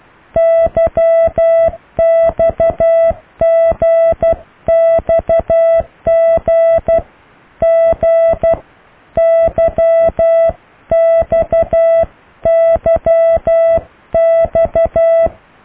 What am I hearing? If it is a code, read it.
Morse code, heard as a keyed tone, about 12 wpm, 660 Hz. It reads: YXGXGGYXYX